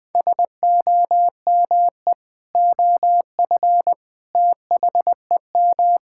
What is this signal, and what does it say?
Morse code, heard as a keyed tone, about 21 wpm, 700 Hz. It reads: SOME OF THEM
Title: Sentences with 3-Word Phrases - 20wpm